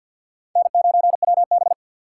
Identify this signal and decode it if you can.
Morse code, heard as a keyed tone, about 49 wpm, 690 Hz. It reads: N9WB
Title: USA Call Signs - 4 Characters - 6x Character Spacing - 50wpm